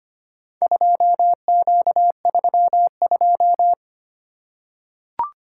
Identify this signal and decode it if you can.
Morse code, heard as a keyed tone, about 25 wpm, 700 Hz. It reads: 2Q32
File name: practice / 25wpm / Four Letter-Number Combinations - 25wpm